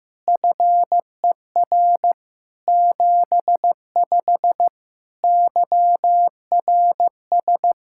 Morse code, heard as a keyed tone, about 15 wpm, 700 Hz. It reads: FER 75 YRS
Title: Practice QSO - Exchange 3 - More Info - 15wpm